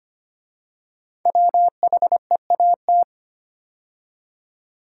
Morse code, heard as a keyed tone, about 25 wpm, 700 Hz. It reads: WHEAT
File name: 5 Letter Words - First 3 Letters Given - 25wpm